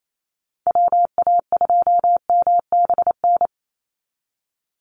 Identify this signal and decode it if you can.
Morse code, heard as a keyed tone, about 28 wpm, 700 Hz. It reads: WA2MBN
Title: USA Call Signs 28wpm